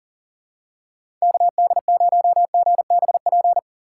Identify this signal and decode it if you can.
Morse code, heard as a keyed tone, about 40 wpm, 700 Hz. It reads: KD0GBP
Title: USA Call Signs - 6x Character Spacing - 40wpm